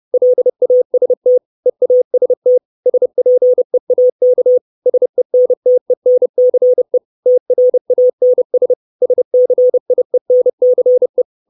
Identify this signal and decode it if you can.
Morse code, heard as a keyed tone, about 30 wpm, 500 Hz. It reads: LAST EAST SPEAK SENTENCE TRANS SCIENCE
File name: 632 Words 30WPM